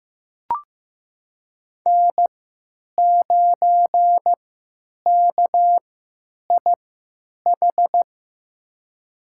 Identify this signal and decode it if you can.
Morse code, heard as a keyed tone, about 15 wpm, spaced as for 9 wpm, 700 Hz. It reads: N9KIH
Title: USA Call Signs - 3x Character Spacing - 15wpm